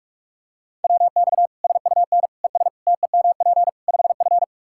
Morse code, heard as a keyed tone, about 46 wpm, 700 Hz. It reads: WX SUN ES TEMP 5F
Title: Practice QSO - Exchange 2 - Info - 45wpm